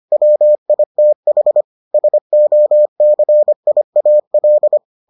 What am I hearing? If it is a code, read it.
Morse code, heard as a keyed tone, about 25 wpm, 600 Hz. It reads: WITH SOCIAL